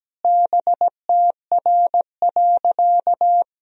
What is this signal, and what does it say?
Morse code, heard as a keyed tone, about 17 wpm, 700 Hz. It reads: BTR.
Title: Sentences from Top 400 Words and Abbreviations 17wpm